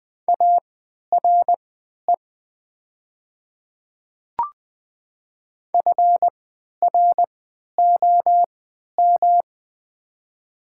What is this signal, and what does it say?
Morse code, heard as a keyed tone, about 20 wpm, spaced as for 11 wpm, 700 Hz. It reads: ARE FROM
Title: 3x Character Spacing - Top 100 Words - 20wpm